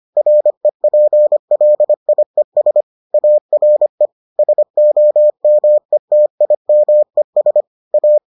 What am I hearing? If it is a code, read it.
Morse code, heard as a keyed tone, about 25 wpm, 600 Hz. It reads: REPLIES ARE SOMETIMES A